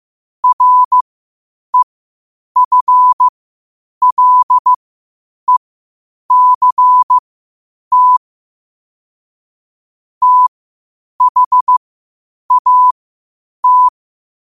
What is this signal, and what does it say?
Morse code, heard as a keyed tone, about 15 wpm, spaced as for 8 wpm, 1000 Hz. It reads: REFLECT THAT
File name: qotdfiles / 8wpm